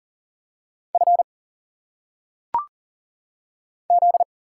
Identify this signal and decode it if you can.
Morse code, heard as a keyed tone, about 40 wpm, 700 Hz. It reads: F Z
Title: Single Letters - Phonetic Pronunciation 40wpm